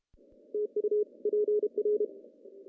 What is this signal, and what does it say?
Morse code, heard as a keyed tone, about 33 wpm, 410 Hz. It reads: TU PR
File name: CW / cqww-cw-k5tr